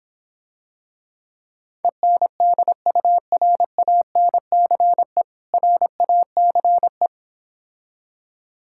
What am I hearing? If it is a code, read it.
Morse code, heard as a keyed tone, about 26 wpm, 700 Hz. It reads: ENDURANCE RACE